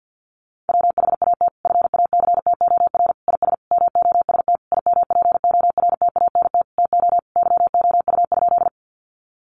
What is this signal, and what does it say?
Morse code, heard as a keyed tone, about 50 wpm, 700 Hz. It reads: WHAT FACTOR IS MOST IMPORTANT TO YOU?
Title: Sentences from Top 1300 Words 50wpm